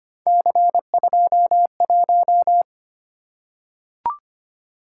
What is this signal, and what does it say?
Morse code, heard as a keyed tone, about 25 wpm, 700 Hz. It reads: C21
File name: Three Letter-Number Combinations - 25wpm